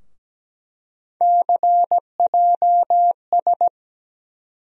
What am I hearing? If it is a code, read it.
Morse code, heard as a keyed tone, about 17 wpm, 700 Hz. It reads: CJS
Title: USA Call Sign Postfixes - 3 Letters 17wpm